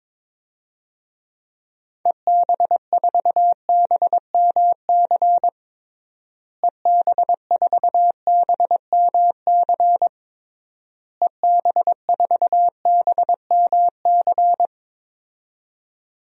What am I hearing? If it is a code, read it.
Morse code, heard as a keyed tone, about 22 wpm, 700 Hz. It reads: EB4BMC EB4BMC EB4BMC
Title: DXCC-100 Call Signs - 6 Characters - Repeated 3x - 22wpm